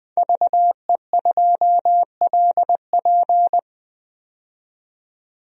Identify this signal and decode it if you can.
Morse code, heard as a keyed tone, about 20 wpm, 700 Hz. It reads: VE2LP
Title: DXCC-100 Call Signs - 5 Characters - 20wpm